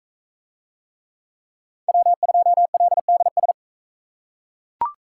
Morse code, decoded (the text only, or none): W2LDS